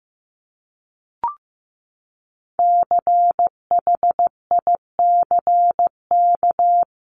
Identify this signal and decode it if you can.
Morse code, heard as a keyed tone, about 15 wpm, 700 Hz. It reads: CHICK
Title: CWT Exchange 15wpm